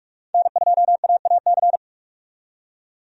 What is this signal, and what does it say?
Morse code, heard as a keyed tone, about 45 wpm, 700 Hz. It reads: N2AAC
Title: USA Call Signs - 5 Characters - 45wpm